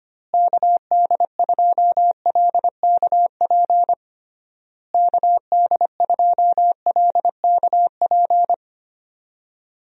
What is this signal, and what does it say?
Morse code, heard as a keyed tone, about 25 wpm, 700 Hz. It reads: KD2LKP KD2LKP
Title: USA Call Signs - 6 Characters - Repeated 3x - 25wpm